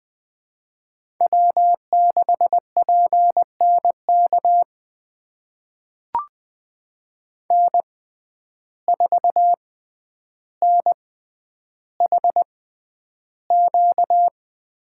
Morse code, heard as a keyed tone, about 20 wpm, spaced as for 10 wpm, 700 Hz. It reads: W6PNK N4NHQ
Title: USA Call Signs - 5 Characters - 6x Character Spacing - 20wpm